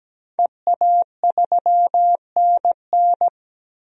Morse code, heard as a keyed tone, about 17 wpm, 700 Hz. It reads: EA3NN
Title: DXCC-100 Call Signs - 5 Characters - Repeated 2x - 17wpm